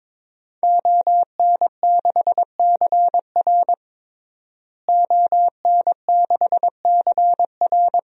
Morse code, heard as a keyed tone, about 22 wpm, 700 Hz. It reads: ON6CR ON6CR
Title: DXCC-100 Call Signs - 5 Characters - Repeated 2x - 22wpm